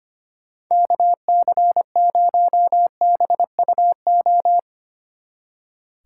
Morse code, heard as a keyed tone, about 30 wpm, 700 Hz. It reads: KC0BUO
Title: Speed Racing - USA 2x3 Call Signs 25wpm